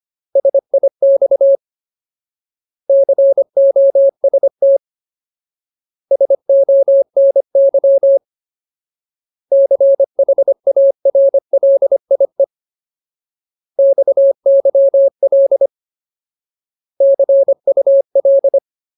Morse code, heard as a keyed tone, about 25 wpm, 550 Hz. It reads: SIX COST SONY CHARLIE XYL CUL
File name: Ultimate Word List / 25WPM Ultimate